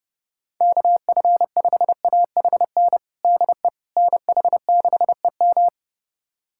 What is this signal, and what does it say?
Morse code, heard as a keyed tone, about 30 wpm, 700 Hz. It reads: KF5AHN DE NH6EM
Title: USA Callsign DE Callsign 30wpm